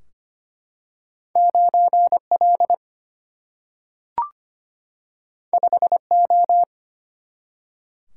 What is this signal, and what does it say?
Morse code, heard as a keyed tone, about 25 wpm, 700 Hz. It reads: 9L 5O
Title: Single-Letter Single-Number 25wpm